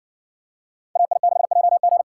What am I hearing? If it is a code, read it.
Morse code, heard as a keyed tone, about 60 wpm, 700 Hz. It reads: AI6JG